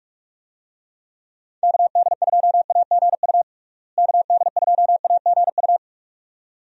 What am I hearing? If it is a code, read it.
Morse code, heard as a keyed tone, about 46 wpm, 700 Hz. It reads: KD2AGU KD2AGU
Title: USA Call Signs - 6 Characters - Repeated 2x - 45wpm